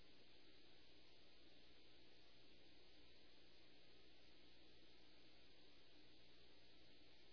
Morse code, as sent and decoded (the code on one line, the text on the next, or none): none